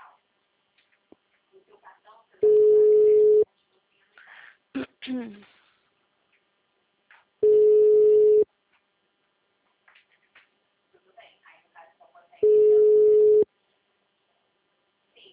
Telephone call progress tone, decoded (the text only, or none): ringback tone